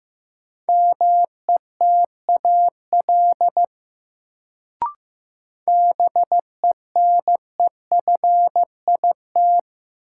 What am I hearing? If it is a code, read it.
Morse code, heard as a keyed tone, about 15 wpm, 700 Hz. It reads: METAL BENEFIT
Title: Sentences from Top 1000 Words - Review Part 6 - New Word Drill - 15wpm